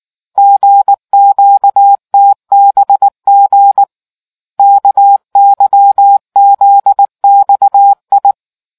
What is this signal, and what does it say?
Morse code, heard as a keyed tone, about 19 wpm, 800 Hz. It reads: GQTBG KYZXI